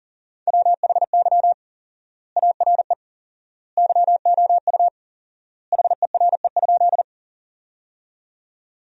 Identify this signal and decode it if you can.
Morse code, heard as a keyed tone, about 40 wpm, 700 Hz. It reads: WHY ARE YOU HERE?
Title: 4x Word-Spacing - Sentences from Top 200 Words - 40wpm